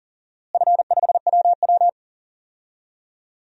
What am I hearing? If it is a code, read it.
Morse code, heard as a keyed tone, about 40 wpm, 700 Hz. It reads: F5WW
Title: DXCC-100 Call Signs - 4 Characters - Repeated 3x - 40wpm